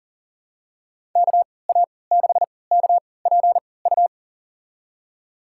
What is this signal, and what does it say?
Morse code, heard as a keyed tone, about 39 wpm, spaced as for 23 wpm, 700 Hz. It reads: KA6KPU
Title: USA Call Signs - 3x Character Spacing - 40wpm